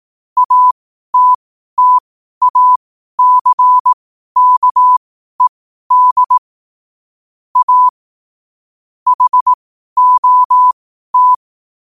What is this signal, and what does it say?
Morse code, heard as a keyed tone, about 18 wpm, spaced as for 12 wpm, 1000 Hz. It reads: ATTACKED A HOT
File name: qotdfiles / 12wpm